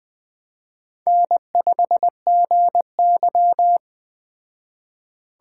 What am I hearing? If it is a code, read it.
Morse code, heard as a keyed tone, about 20 wpm, 700 Hz. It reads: N5GY
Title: Speed Racing - USA 1x2 Call Signs 17wpm